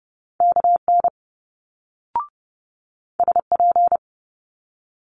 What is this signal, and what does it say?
Morse code, heard as a keyed tone, about 30 wpm, 700 Hz. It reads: KN SP
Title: Multi-letter Phonemes Quick Fire 30wpm